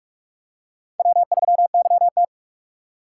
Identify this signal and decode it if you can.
Morse code, heard as a keyed tone, about 45 wpm, 700 Hz. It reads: W3YT